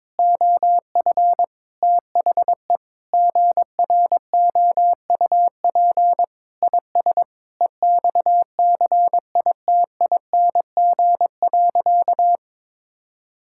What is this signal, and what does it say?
Morse code, heard as a keyed tone, about 22 wpm, 700 Hz. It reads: OF THE GROUP IS EXCITING.
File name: practice / 22wpm / Sentences with 4-Word Phrases - 22wpm